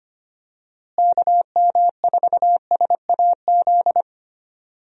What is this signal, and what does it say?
Morse code, heard as a keyed tone, about 25 wpm, 700 Hz. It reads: KM4SAZ
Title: USA Call Signs - 6 Characters - Repeated 2x - 25wpm